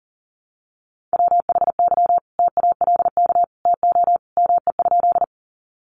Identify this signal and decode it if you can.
Morse code, heard as a keyed tone, about 40 wpm, 700 Hz. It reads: WHY TALK TO ME?
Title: Sentences from Top 400 Words 40wpm